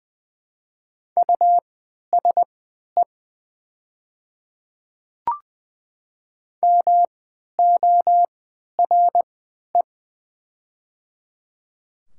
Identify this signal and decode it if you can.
Morse code, heard as a keyed tone, about 20 wpm, spaced as for 11 wpm, 700 Hz. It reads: USE MORE
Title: 3x Character Spacing - Top 500 Words - 20wpm